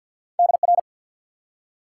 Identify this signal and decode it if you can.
Morse code, heard as a keyed tone, about 51 wpm, 700 Hz. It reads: DR
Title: Top 90 QSO Elements 50wpm